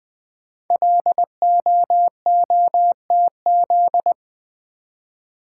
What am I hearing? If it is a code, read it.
Morse code, heard as a keyed tone, about 20 wpm, 700 Hz. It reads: LOOTZ
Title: Code Groups - 5 Letters - 20wpm